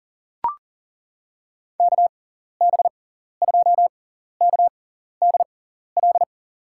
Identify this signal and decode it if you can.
Morse code, heard as a keyed tone, about 40 wpm, spaced as for 14 wpm, 700 Hz. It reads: KB2KDL
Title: USA Call Signs - 6x Character Spacing - 40wpm